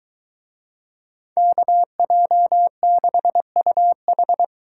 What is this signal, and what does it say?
Morse code, heard as a keyed tone, about 22 wpm, 700 Hz. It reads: KJ6UH